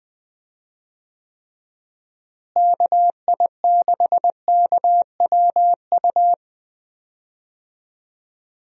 Morse code, heard as a keyed tone, about 20 wpm, 700 Hz. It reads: KI6KWU
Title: USA Call Signs - 6 Characters - Repeated 2x - 20wpm